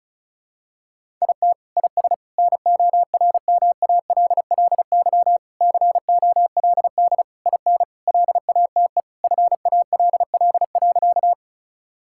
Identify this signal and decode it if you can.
Morse code, heard as a keyed tone, about 34 wpm, 700 Hz. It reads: IT IS NORMALLY COLD IN LATE FALL.
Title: Sentences from Top 1000 Words - Review Part 6 - 35wpm